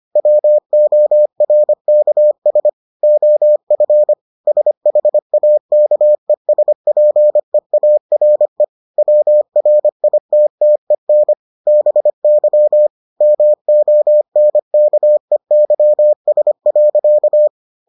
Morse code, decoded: WORKS OF SHAKESPEARE WRITTEN BY MONKEYS.

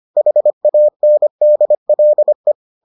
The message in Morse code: HANDLE

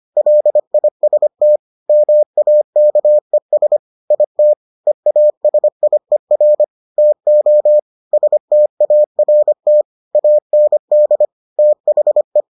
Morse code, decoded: LIST MAKES IT EASIER TO START AND THE

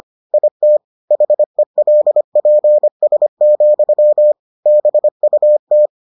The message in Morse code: IT HELPS, BUT

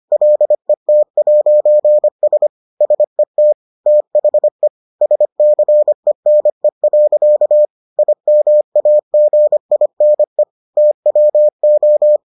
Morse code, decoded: LET'S SET THE SCENE. IMAGINE TWO

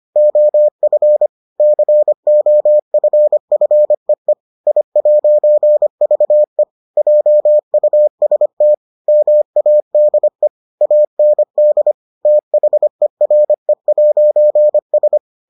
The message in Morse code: OF COFFEE I'VE JUST MADE AND THERE'S

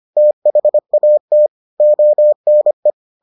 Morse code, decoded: THAT ONE

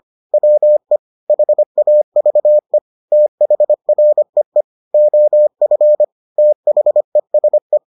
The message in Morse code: WE HAVE THREE OF THESE

